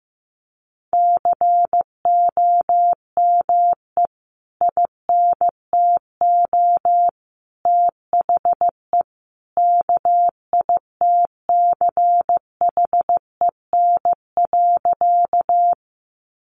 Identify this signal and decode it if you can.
Morse code, heard as a keyed tone, about 14 wpm, 700 Hz. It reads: COME INTO THE KITCHEN.
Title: Sentences from Top 1400 Words 15wpm